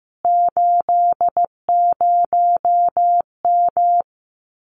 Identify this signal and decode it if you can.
Morse code, heard as a keyed tone, about 15 wpm, 700 Hz. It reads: 80M
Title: Common QSO Phrases Encore 15wpm